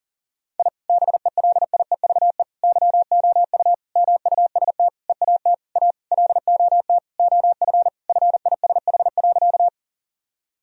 Morse code, heard as a keyed tone, about 40 wpm, 700 Hz. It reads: I BELIEVE YOU MUST EAT A LOT OF FISH.
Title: Sentences from Top 1000 Words - Review Part 3 - 40wpm